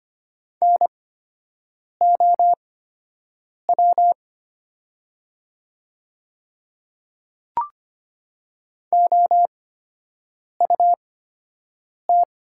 Morse code, heard as a keyed tone, about 25 wpm, spaced as for 7 wpm, 700 Hz. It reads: NOW OUT
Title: 8x Character Spacing - Top 100 Words - 25wpm